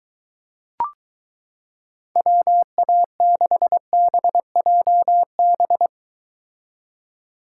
Morse code, heard as a keyed tone, about 22 wpm, 700 Hz. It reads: WA6BJB